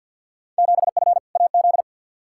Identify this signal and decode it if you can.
Morse code, heard as a keyed tone, about 52 wpm, 700 Hz. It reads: 7F AZ
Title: Field Day Exchanges Encore - 50wpm